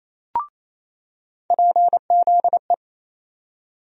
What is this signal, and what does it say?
Morse code, heard as a keyed tone, about 27 wpm, 700 Hz. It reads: PZE
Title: USA Call Sign Postfixes - 3 Letters 28wpm